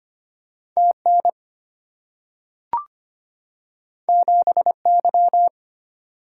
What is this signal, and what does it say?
Morse code, heard as a keyed tone, about 25 wpm, 700 Hz. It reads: TN 7Y
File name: Two Letter-Number Combinations - 25wpm